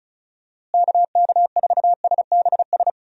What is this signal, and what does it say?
Morse code, heard as a keyed tone, about 34 wpm, 700 Hz. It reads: KK4SBS
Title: USA Call Signs - 6 Characters - Repeated 3x - 35wpm